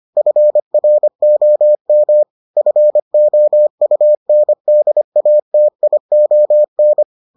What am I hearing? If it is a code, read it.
Morse code, heard as a keyed tone, about 25 wpm, 600 Hz. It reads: FROM FOUNDATION